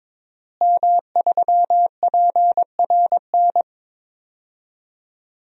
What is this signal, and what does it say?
Morse code, heard as a keyed tone, about 21 wpm, 700 Hz. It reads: M3PRN
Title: DXCC-100 Call Signs - 5 Characters - Repeated 2x - 22wpm